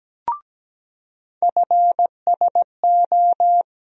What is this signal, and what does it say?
Morse code, heard as a keyed tone, about 17 wpm, 700 Hz. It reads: FSO
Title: USA Call Sign Postfixes - 3 Letters 17wpm